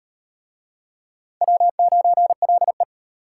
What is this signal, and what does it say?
Morse code, heard as a keyed tone, about 34 wpm, 700 Hz. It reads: W9LE